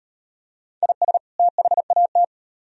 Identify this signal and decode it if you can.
Morse code, heard as a keyed tone, about 38 wpm, 700 Hz. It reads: IS THAT